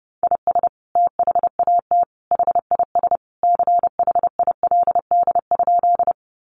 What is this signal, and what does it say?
Morse code, heard as a keyed tone, about 30 wpm, 700 Hz. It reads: IS THAT HIS CHILD?
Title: Sentences from Top 800 Words 30wpm